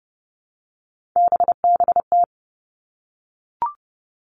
Morse code, heard as a keyed tone, about 31 wpm, 700 Hz. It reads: BBT
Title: Three Challenging Letter-Number Combinations 30wpm